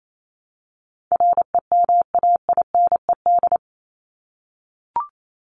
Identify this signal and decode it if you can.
Morse code, heard as a keyed tone, about 28 wpm, 700 Hz. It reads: REMAINED
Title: ED Suffix - Top 100 Words with Base Words 28wpm